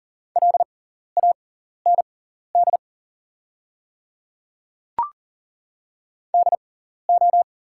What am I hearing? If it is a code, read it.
Morse code, heard as a keyed tone, about 40 wpm, spaced as for 14 wpm, 700 Hz. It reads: LAND DO
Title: 6x Character Spacing - Top 200 Words - 40wpm